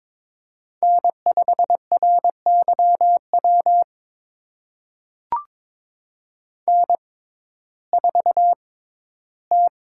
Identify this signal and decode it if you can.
Morse code, heard as a keyed tone, about 22 wpm, spaced as for 13 wpm, 700 Hz. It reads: N5RYW N4T
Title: USA Call Signs - 5 Characters - 6x Character Spacing - 22wpm